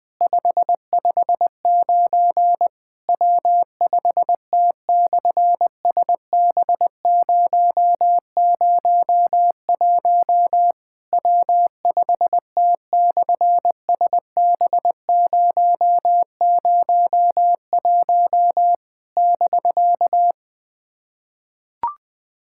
Morse code, heard as a keyed tone, about 19 wpm, 700 Hz. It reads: 559 W5T/SB001 W5T/SB001 <BK>
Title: SOTA QSO Practice 20wpm